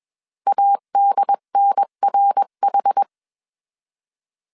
Morse code, heard as a keyed tone, about 21 wpm, 800 Hz. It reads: ADNRH